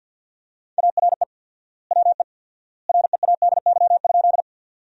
Morse code, heard as a keyed tone, about 50 wpm, 700 Hz. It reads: ARE WE READY?
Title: 4x Word-Spacing - Sentences from Top 400 Words - 50wpm